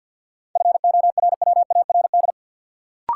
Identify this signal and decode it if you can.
Morse code, heard as a keyed tone, about 52 wpm, 700 Hz. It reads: FORWARD
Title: Sentences from Top 1300 Words - New Word Drill Alt - 50wpm